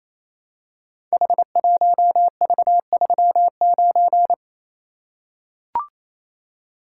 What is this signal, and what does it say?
Morse code, heard as a keyed tone, about 28 wpm, 700 Hz. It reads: H1V39
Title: Code Groups - 5 Letter:Numbers - 28wpm